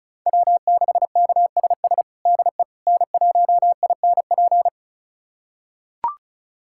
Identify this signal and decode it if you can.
Morse code, heard as a keyed tone, about 36 wpm, 700 Hz. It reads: W6KSS DE N1INP
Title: USA Callsign DE Callsign - 5 Characters - 35wpm